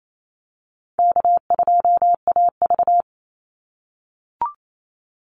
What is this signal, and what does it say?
Morse code, heard as a keyed tone, about 28 wpm, 700 Hz. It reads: K2AV
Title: Call Signs - 28wpm